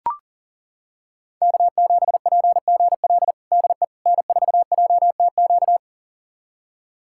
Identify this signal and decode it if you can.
Morse code, heard as a keyed tone, about 40 wpm, 700 Hz. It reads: K7PGL DE N4JTQ
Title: USA Callsign DE Callsign - 5 Characters - 40wpm